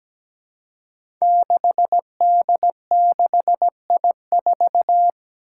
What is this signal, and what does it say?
Morse code, heard as a keyed tone, about 17 wpm, 700 Hz. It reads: 6D6I4
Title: Five Challenging Letter-Number Combinations 17wpm